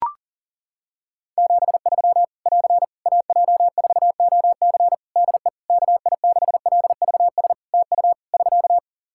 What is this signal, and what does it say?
Morse code, heard as a keyed tone, about 37 wpm, 700 Hz. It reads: 73 <AR> AJ4OC DE KI6LVS TU <SK>
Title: Practice QSO - Exchange 4 - Ending - 40wpm